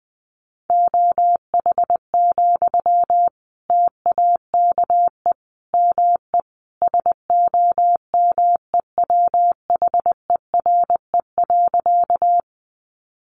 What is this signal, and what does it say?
Morse code, heard as a keyed tone, about 20 wpm, 700 Hz. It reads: OH, TAKE ME SOMEWHERE.
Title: Sentences from Top 1400 Words 20wpm